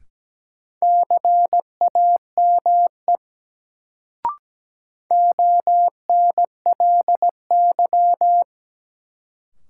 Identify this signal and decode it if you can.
Morse code, heard as a keyed tone, about 17 wpm, 700 Hz. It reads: CAME ONLY
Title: Sentences from Top 200 Words - New Word Drill - 17wpm